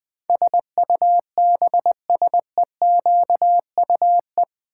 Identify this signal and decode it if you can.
Morse code, heard as a keyed tone, about 20 wpm, 700 Hz. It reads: SUBSEQUE
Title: Build A Word - 4 Letters At A Time - 12-Letter Words - 20wpm